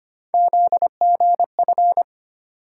Morse code, heard as a keyed tone, about 25 wpm, 700 Hz. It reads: ZGF
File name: Three Letter-Number Combinations - 25wpm